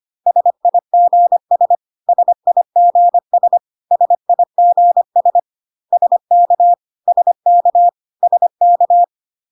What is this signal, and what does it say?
Morse code, heard as a keyed tone, about 25 wpm, 700 Hz. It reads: SIGS SIGS SIGS SK SK SK
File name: CW abbreviations alphabetical repeated 3 times 25 25wpm ik1hge